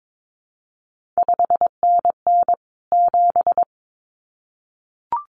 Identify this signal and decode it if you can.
Morse code, heard as a keyed tone, about 22 wpm, 700 Hz. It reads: 5NN 7
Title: CQ WW DX Contest Exchange Encore - 22wpm